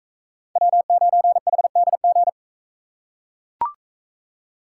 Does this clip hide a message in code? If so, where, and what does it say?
Yes: Morse code from 0.6 to 2.3 s, heard as a keyed tone, about 40 wpm, 700 Hz. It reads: W9HDG